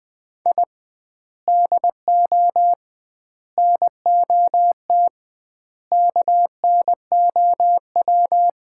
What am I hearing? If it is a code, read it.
Morse code, heard as a keyed tone, about 21 wpm, 700 Hz. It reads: I DO NOT KNOW
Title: 2x Word-Spacing - Sets of 5 Words - 20wpm